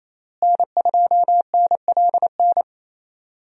Morse code, heard as a keyed tone, about 28 wpm, 700 Hz. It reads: N2NLN
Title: USA Call Signs - 5 Characters - Repeated 2x - 28wpm